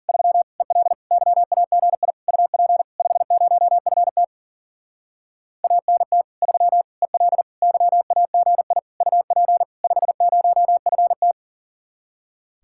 Morse code, heard as a keyed tone, about 42 wpm, 700 Hz. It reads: 3 EL YAGI UP 50FT ANT 3 EL YAGI UP 50FT
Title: Speed-Racing Common QSO Phrases - 40wpm